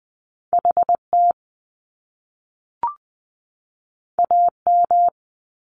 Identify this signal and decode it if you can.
Morse code, heard as a keyed tone, about 20 wpm, 700 Hz. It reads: HT AM
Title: Top Two Letter Combinations in Words 20wpm